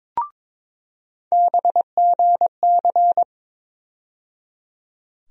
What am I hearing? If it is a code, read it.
Morse code, heard as a keyed tone, about 21 wpm, 700 Hz. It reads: BGC
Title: USA Call Sign Postfixes - 3 Letters 22wpm